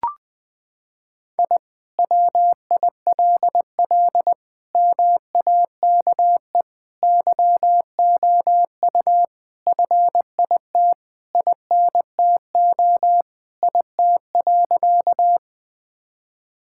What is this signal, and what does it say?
Morse code, heard as a keyed tone, about 21 wpm, 700 Hz. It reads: I WILL MAKE YOU FIT INTO IT.
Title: Sentences from Top 1000 Words - Review Part 5 - 20wpm